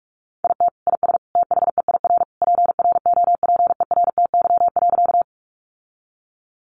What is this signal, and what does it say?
Morse code, heard as a keyed tone, about 44 wpm, 700 Hz. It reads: IT IS THEIR PROPERTY.
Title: Sentences from Top 1000 Words 45wpm